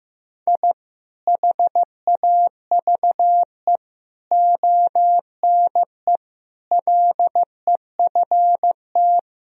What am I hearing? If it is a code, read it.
Morse code, heard as a keyed tone, about 15 wpm, 700 Hz. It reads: I HAVE ONE LEFT
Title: Sentences from Top 1000 Words - Review Part 4 - 15wpm